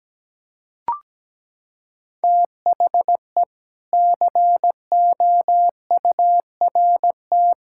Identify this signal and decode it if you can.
Morse code, heard as a keyed tone, about 17 wpm, 700 Hz. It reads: THE COURT
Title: Sentences from Top 1600 Words 17wpm